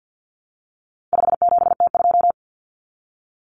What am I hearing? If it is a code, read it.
Morse code, heard as a keyed tone, about 50 wpm, 700 Hz. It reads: 57T2
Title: Common Cut Numbers - 4-Digit 50wpm